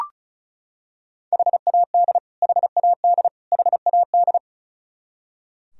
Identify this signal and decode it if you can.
Morse code, heard as a keyed tone, about 34 wpm, 700 Hz. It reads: HAD HAD HAD
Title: Top 100 Words Repeated 35wpm